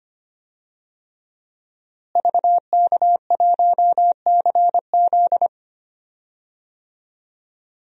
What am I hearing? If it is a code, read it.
Morse code, heard as a keyed tone, about 25 wpm, 700 Hz. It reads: VK1CZ
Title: DXCC-100 Call Signs - 5 Characters - Repeated 3x - 25wpm